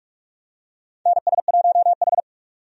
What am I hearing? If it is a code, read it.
Morse code, heard as a keyed tone, about 46 wpm, 700 Hz. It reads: NS1H
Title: USA Call Signs - 4 Characters - 45wpm